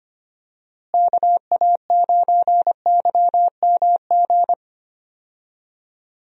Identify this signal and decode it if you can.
Morse code, heard as a keyed tone, about 25 wpm, 700 Hz. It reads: KA9YMG
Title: USA Call Signs - 6 Characters - Repeated 3x - 25wpm